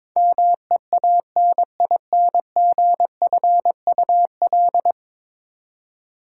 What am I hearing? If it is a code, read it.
Morse code, heard as a keyed tone, about 21 wpm, 700 Hz. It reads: MEANINGFUL
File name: practice / 22wpm / 10 Letter Words - First 3 Letters Given - 22wpm